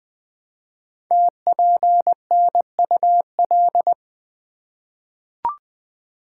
Morse code, heard as a keyed tone, about 20 wpm, 700 Hz. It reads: TPNUL
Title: Code Groups - 5 Letters - 20wpm